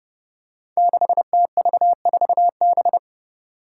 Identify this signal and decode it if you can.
Morse code, heard as a keyed tone, about 30 wpm, 700 Hz. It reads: 6TV4B
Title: Five Challenging Letter-Number Combinations 30wpm